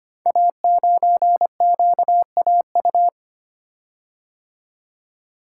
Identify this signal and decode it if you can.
Morse code, heard as a keyed tone, about 25 wpm, 700 Hz. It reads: A9QAU
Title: Code Groups - 5 Letter:Numbers - 25wpm